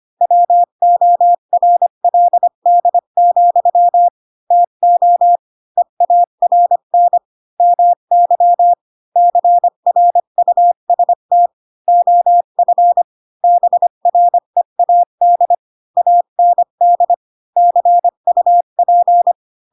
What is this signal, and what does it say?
Morse code, heard as a keyed tone, about 25 wpm, 700 Hz. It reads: WORLD, TO EARN MY CRUST OF BREAD AND CUP